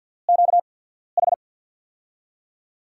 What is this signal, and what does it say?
Morse code, heard as a keyed tone, about 54 wpm, spaced as for 14 wpm, 700 Hz. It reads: QH
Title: USA Call Signs - 5 Characters - 8x Character Spacing - 50wpm